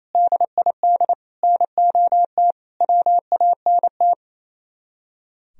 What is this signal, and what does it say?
Morse code, heard as a keyed tone, about 35 wpm, 700 Hz. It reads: DID NOT WANT